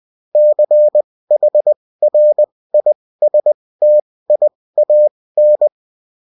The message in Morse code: CHRISTIAN